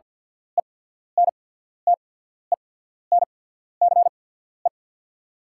Morse code, heard as a keyed tone, about 48 wpm, spaced as for 14 wpm, 700 Hz. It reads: ENTENCE
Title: 8x Character Spacing - Top 300 Words - 50wpm